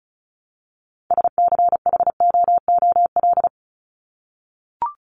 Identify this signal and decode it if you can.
Morse code, heard as a keyed tone, about 34 wpm, 700 Hz. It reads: SCHOOL
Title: Top 300 Words 35wpm